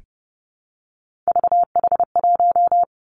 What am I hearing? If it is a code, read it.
Morse code, heard as a keyed tone, about 31 wpm, 700 Hz. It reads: VH1
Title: Three Challenging Letter-Number Combinations 30wpm